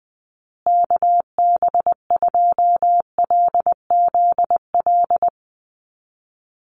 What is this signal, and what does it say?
Morse code, heard as a keyed tone, about 20 wpm, 700 Hz. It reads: KB2LZL
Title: USA Call Signs Repeated 3x 20wpm